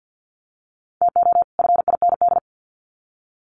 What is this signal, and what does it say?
Morse code, heard as a keyed tone, about 50 wpm, 700 Hz. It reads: TO FIND
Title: Sets of 2 Words 50wpm